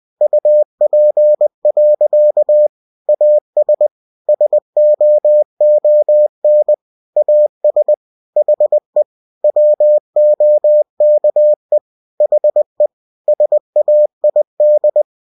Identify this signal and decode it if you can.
Morse code, heard as a keyed tone, about 20 wpm, 600 Hz. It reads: UP. AS SOON AS HE WOKE HE SAID
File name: cw-books / winnie-the-pooh / winnie-20wpm-page-0040